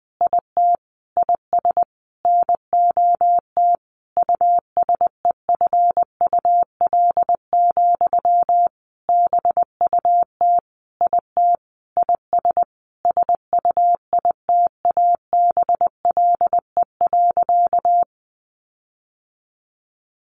Morse code, heard as a keyed tone, about 20 wpm, 700 Hz. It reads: IT IS NOT USEFUL, BUT IT IS SUITABLE.
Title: Sentences from Top 1400 Words 20wpm